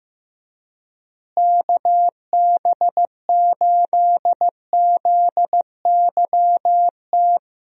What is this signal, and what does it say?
Morse code, heard as a keyed tone, about 15 wpm, 700 Hz. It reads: KB8ZYT
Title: USA Call Signs - 6 Characters - Repeated 3x - 15wpm